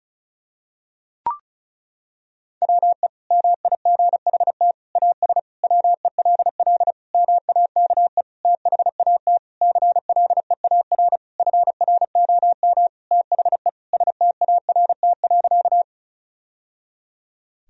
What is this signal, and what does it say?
Morse code, heard as a keyed tone, about 36 wpm, 700 Hz. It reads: WE MIGHT AS WELL MAKE THAT CLEAR FROM THE START.
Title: Sentences from Top 1000 Words - Review Part 1 - 35wpm